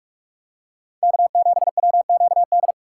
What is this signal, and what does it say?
Morse code, heard as a keyed tone, about 47 wpm, 700 Hz. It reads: K7WQD
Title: USA Call Signs - 6x Character Spacing - 45wpm